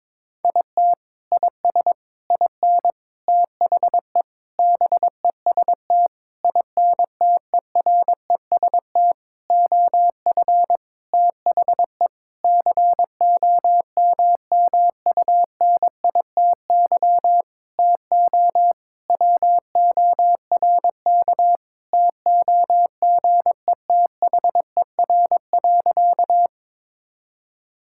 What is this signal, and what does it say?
Morse code, heard as a keyed tone, about 22 wpm, 700 Hz. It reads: IT IS IN THE BEST INTEREST OF THE COMMUNITY TO WORK TOGETHER.
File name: Sentences with 7-Word Phrases - 22wpm